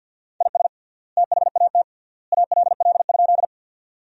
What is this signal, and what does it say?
Morse code, heard as a keyed tone, about 50 wpm, 700 Hz. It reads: IS THAT ALL?
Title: 3x Word-Spacing - Sentences from Top 100 Words - 50wpm